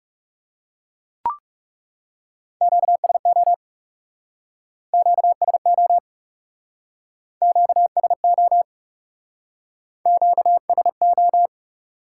Morse code, heard as a keyed tone, about 40 wpm, 700 Hz. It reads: QSO QSO QSO Q S O